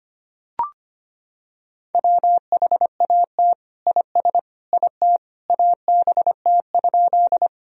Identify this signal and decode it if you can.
Morse code, heard as a keyed tone, about 25 wpm, 700 Hz. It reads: WHAT IS IT ABT?
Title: Sentences from Top 100 Words and Abbreviations 25wpm